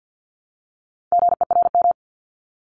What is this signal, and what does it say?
Morse code, heard as a keyed tone, about 50 wpm, 700 Hz. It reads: GERM
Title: Multi-letter Phonemes Repeated 3x 50wpm